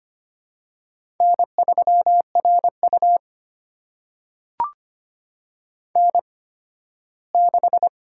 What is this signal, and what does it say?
Morse code, heard as a keyed tone, about 25 wpm, spaced as for 15 wpm, 700 Hz. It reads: N3RU N6
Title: USA Call Signs - 4 Characters - 8x Character Spacing - 25wpm